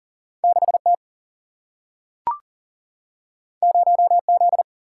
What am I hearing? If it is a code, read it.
Morse code, heard as a keyed tone, about 40 wpm, 700 Hz. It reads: 6 T 0 Z
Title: Single-Letter Single-Number 40wpm